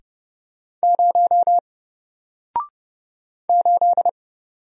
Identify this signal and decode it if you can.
Morse code, heard as a keyed tone, about 30 wpm, 700 Hz. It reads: 0 8
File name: Single-Letter-Number-30wpm